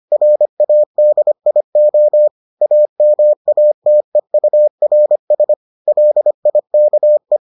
Morse code, heard as a keyed tone, about 25 wpm, 600 Hz. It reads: RADIO AMATEURS LIKE